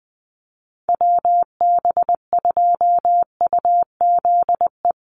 Morse code, heard as a keyed tone, about 20 wpm, 700 Hz. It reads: WB2UZE
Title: Call Signs - 20wpm